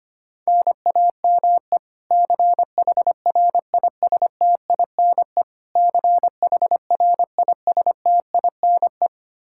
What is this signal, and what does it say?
Morse code, heard as a keyed tone, about 25 wpm, 700 Hz. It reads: NAME CHRISTINE CHRISTINE
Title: Practice QSO - Exchange 1 - Beginning - 25wpm